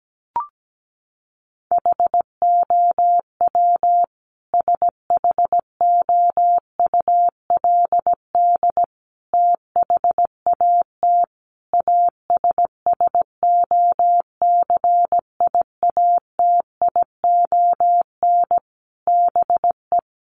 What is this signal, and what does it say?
Morse code, heard as a keyed tone, about 17 wpm, 700 Hz. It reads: HOW SHOULD THAT ASSOCIATION BE
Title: Sentences from Top 1300 Words 17wpm